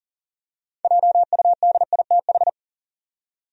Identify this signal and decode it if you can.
Morse code, heard as a keyed tone, about 40 wpm, 700 Hz. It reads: JUDITH
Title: USA Common Names - 40wpm